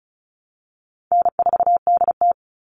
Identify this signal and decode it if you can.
Morse code, heard as a keyed tone, about 34 wpm, 700 Hz. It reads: N4DT
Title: Call Signs Repeated 2x - 35wpm